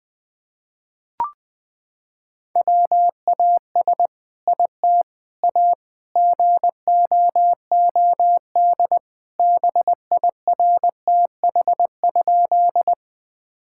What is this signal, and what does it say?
Morse code, heard as a keyed tone, about 20 wpm, 700 Hz. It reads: WAS IT A GOOD BIRTH?
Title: Sentences from Top 1000 Words - Review Part 5 - 20wpm